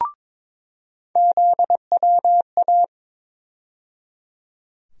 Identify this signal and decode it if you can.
Morse code, heard as a keyed tone, about 22 wpm, 700 Hz. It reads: ZWA
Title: USA Call Sign Postfixes - 3 Letters 22wpm